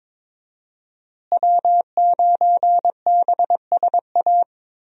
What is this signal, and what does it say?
Morse code, heard as a keyed tone, about 22 wpm, 700 Hz. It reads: W9BSA
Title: USA Call Signs - 5 Characters - Repeated 2x - 22wpm